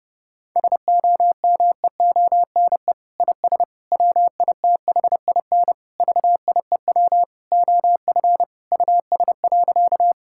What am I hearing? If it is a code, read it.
Morse code, heard as a keyed tone, about 28 wpm, 700 Hz. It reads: SOMEONE IS WITHIN VIEW OF US.
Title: Sentences from Top 1000 Words - Review Part 6 - 30wpm